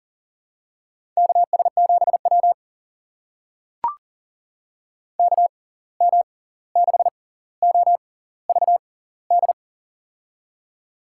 Morse code, heard as a keyed tone, about 38 wpm, spaced as for 18 wpm, 700 Hz. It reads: KS7W KM6OVD
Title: USA Call Signs - 6x Character Spacing - 40wpm